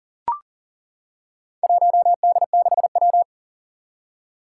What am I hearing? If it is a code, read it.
Morse code, heard as a keyed tone, about 39 wpm, 700 Hz. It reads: 1D6W
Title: Four Challenging Letter-Number Combinations 40wpm